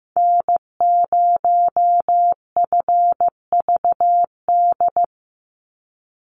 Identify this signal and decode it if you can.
Morse code, heard as a keyed tone, about 15 wpm, 700 Hz. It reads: N0FVD
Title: USA Call Signs 15wpm